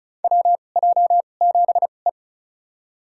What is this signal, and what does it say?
Morse code, heard as a keyed tone, about 35 wpm, spaced as for 25 wpm, 700 Hz. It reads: WJ7E
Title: USA Call Signs - 4 Characters - 2x Character Spacing - 35wpm